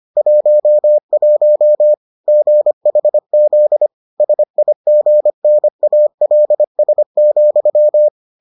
Morse code, heard as a keyed tone, about 25 wpm, 600 Hz. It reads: 11 GHZ SIGNALS,